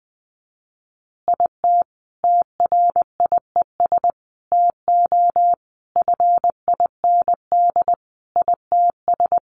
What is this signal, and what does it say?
Morse code, heard as a keyed tone, about 20 wpm, 700 Hz. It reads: IT TRIES TO FIND ITS
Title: Additional Easy Phrases 20wpm